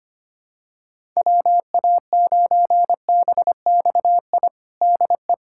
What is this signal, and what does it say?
Morse code, heard as a keyed tone, about 25 wpm, 700 Hz. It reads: WA9BXI DE
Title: K1USN SST QSO Practice 25wpm